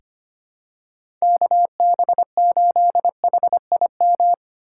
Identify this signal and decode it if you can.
Morse code, heard as a keyed tone, about 25 wpm, 700 Hz. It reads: KB8HIM
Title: USA Call Signs - 2x Character Spacing - 25wpm